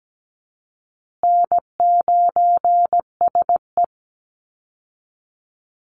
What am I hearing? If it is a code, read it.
Morse code, heard as a keyed tone, about 17 wpm, 700 Hz. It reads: N9SE
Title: Easy Call Signs Repeated 3x - 17wpm